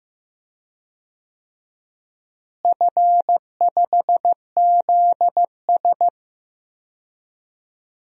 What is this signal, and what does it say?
Morse code, heard as a keyed tone, about 15 wpm, 700 Hz. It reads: F5ZS
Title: DXCC-100 Call Signs - 4 Characters - Repeated 2x - 15wpm